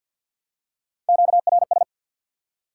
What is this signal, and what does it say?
Morse code, heard as a keyed tone, about 53 wpm, 700 Hz. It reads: QRS
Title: Top 180 QSO Elements 50wpm